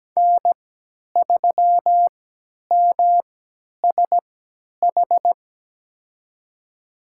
Morse code, heard as a keyed tone, about 17 wpm, spaced as for 10 wpm, 700 Hz. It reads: N3MSH
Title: USA Call Signs - 5 Characters - 3x Character Spacing - 17wpm